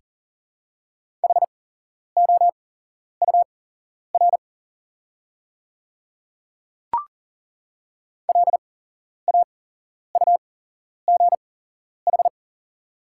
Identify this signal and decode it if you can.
Morse code, heard as a keyed tone, about 40 wpm, spaced as for 11 wpm, 700 Hz. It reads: HOUR LAUGH
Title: 8x Character Spacing - Top 500 Words - 40wpm